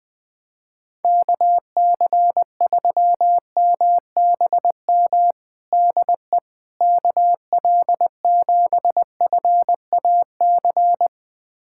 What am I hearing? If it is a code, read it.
Morse code, heard as a keyed tone, about 20 wpm, 700 Hz. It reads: KC3MBM DE KL7FAC
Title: USA Callsign DE Callsign 20wpm